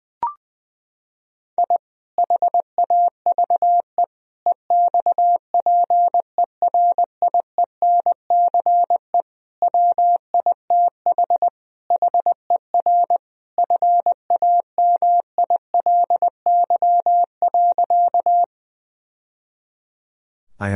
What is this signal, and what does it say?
Morse code, heard as a keyed tone, about 20 wpm, 700 Hz. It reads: I HAVE EXPERIENCE WITH HER FAMILY.
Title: Sentences from Top 1000 Words - Review Part 3 - 20wpm